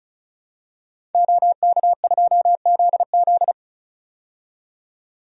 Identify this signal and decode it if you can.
Morse code, heard as a keyed tone, about 34 wpm, 700 Hz. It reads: OK2ZZ
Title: DXCC-100 Call Signs - 5 Characters - 35wpm